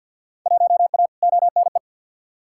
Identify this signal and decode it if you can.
Morse code, heard as a keyed tone, about 46 wpm, 700 Hz. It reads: 1A ONE
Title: Field Day Exchanges - 50wpm